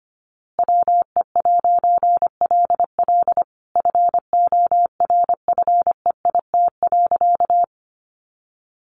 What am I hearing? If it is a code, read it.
Morse code, heard as a keyed tone, about 25 wpm, 700 Hz. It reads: WE'LL FORFEIT.
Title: A Sound Of Thunder 25wpm